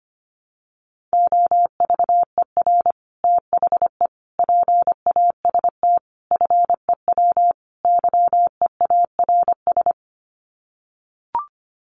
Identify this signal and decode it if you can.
Morse code, heard as a keyed tone, about 25 wpm, 700 Hz. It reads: OVER THE PAST FEW YEARS
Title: Sets of 5 Words Encore 25wpm